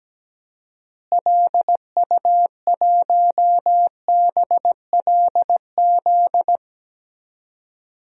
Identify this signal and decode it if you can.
Morse code, heard as a keyed tone, about 17 wpm, 700 Hz. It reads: LU1BLZ
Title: DXCC-100 Call Signs - 6 Characters - Repeated 3x - 17wpm